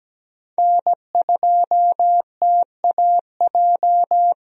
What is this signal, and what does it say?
Morse code, heard as a keyed tone, about 17 wpm, 700 Hz. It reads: N2TAJ
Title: USA Call Signs - 5 Characters - Repeated 2x - 17wpm